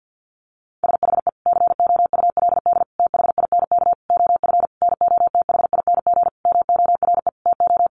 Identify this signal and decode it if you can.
Morse code, heard as a keyed tone, about 50 wpm, 700 Hz. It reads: SHE COULD THINK OF NOTHING MORE TO
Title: Sentences from Top 500 Words 50wpm